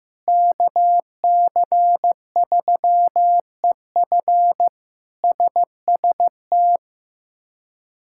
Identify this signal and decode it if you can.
Morse code, heard as a keyed tone, about 15 wpm, 700 Hz. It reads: KC3EF SST
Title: K1USN SST QSO Practice - 15wpm